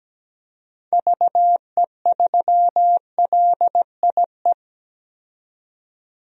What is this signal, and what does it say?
Morse code, heard as a keyed tone, about 17 wpm, 700 Hz. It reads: VE3LIE
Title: DXCC-100 Call Signs - 6 Characters - 17wpm